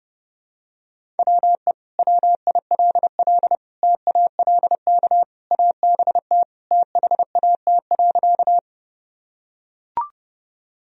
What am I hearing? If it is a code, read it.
Morse code, heard as a keyed tone, about 32 wpm, 700 Hz. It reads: WE WILL TALK ABT THAT.
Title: Sentences from Top 400 Words and Abbreviations 30wpm